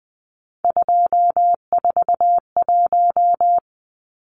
Morse code, heard as a keyed tone, about 20 wpm, 700 Hz. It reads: 241
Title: Numbers - 3-Digit 20wpm